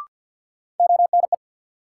Morse code, heard as a keyed tone, about 52 wpm, 700 Hz. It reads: ONE
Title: Sentences from Top 100 Words - New Word Drill - 50wpm